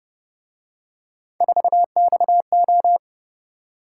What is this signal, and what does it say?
Morse code, heard as a keyed tone, about 28 wpm, 700 Hz. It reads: 4XO